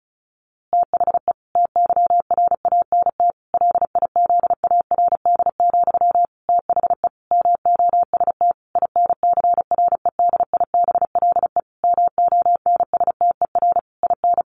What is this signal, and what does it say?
Morse code, heard as a keyed tone, about 34 wpm, 700 Hz. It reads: THE TYRANT LIZARD, THE MOST INCREDIBLE MONSTER IN
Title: A Sound Of Thunder 35wpm